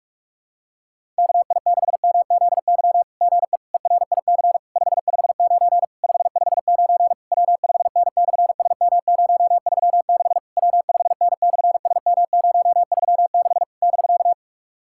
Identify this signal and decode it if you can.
Morse code, heard as a keyed tone, about 46 wpm, 700 Hz. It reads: KI6MZY GE ERIC 559 559 W5N/SM036 W5N/SM036 <BK>
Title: SOTA QSO Practice 45wpm